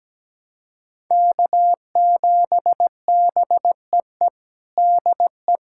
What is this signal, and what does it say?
Morse code, heard as a keyed tone, about 17 wpm, 700 Hz. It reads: K7BEE DE
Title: USA Callsign DE Callsign - 5 Characters - 17wpm